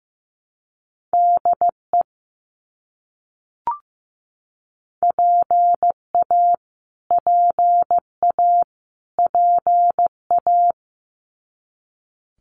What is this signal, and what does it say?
Morse code, heard as a keyed tone, about 15 wpm, 700 Hz. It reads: DE PA PA PA
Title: US States Repeated 3x 15wpm